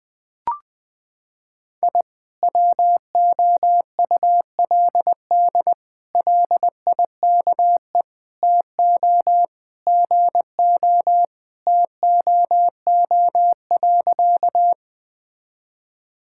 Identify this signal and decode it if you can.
Morse code, heard as a keyed tone, about 20 wpm, 700 Hz. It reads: I WOULD LIKE TO GO TOO.
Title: Sentences with 3-Word Phrases - 20wpm